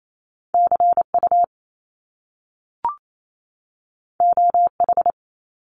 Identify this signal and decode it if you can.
Morse code, heard as a keyed tone, about 28 wpm, 700 Hz. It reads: CU OH
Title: Top Two Letter Combinations in Words 28wpm